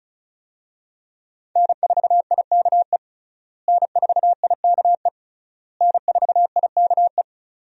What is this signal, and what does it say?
Morse code, heard as a keyed tone, about 34 wpm, 700 Hz. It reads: N4IKE N4IKE N4IKE
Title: USA Call Signs - 5 Characters - Repeated 3x - 35wpm